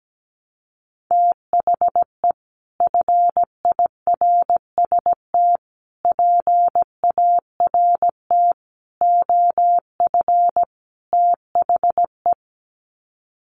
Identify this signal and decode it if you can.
Morse code, heard as a keyed tone, about 17 wpm, 700 Hz. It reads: THE FIRST PART OF THE
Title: Sets of 5 Words Encore 17wpm